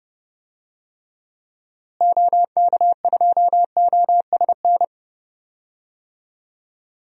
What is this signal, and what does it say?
Morse code, heard as a keyed tone, about 30 wpm, 700 Hz. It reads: OK2OSN
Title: DXCC-100 Call Signs - 6 Characters - Repeated 3x - 30wpm